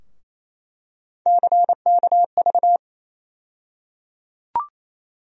Morse code, heard as a keyed tone, about 28 wpm, 700 Hz. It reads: CKV
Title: USA Call Sign Postfixes - 3 Letters 28wpm